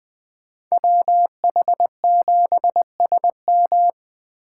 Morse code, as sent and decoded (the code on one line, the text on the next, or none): .-- .... --... ... --
WH7SM